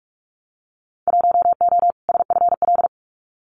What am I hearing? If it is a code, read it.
Morse code, heard as a keyed tone, about 46 wpm, 700 Hz. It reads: 1O SFL
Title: Winter Field Day Exchanges Encore 45wpm